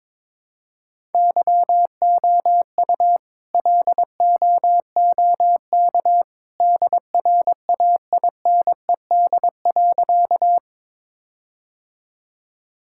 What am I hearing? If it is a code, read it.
Morse code, heard as a keyed tone, about 22 wpm, 700 Hz. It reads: YOU LOOK DRAINED.
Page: morse-code-ninja-practice-22wpm